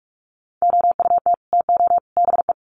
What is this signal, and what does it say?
Morse code, heard as a keyed tone, about 44 wpm, 700 Hz. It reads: OUT TO BE
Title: Sets of 3 Words 45wpm